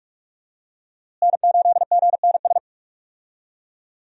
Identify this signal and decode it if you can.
Morse code, heard as a keyed tone, about 47 wpm, 700 Hz. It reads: N8GNS
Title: USA Call Signs - 5 Characters - Repeated 2x - 45wpm